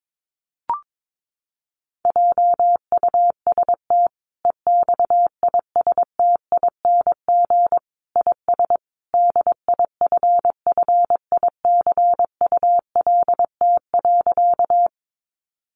Morse code, heard as a keyed tone, about 22 wpm, 700 Hz. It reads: JUST EXISTING IS DIFFICULT.
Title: Sentences from Top 1500 Words 22wpm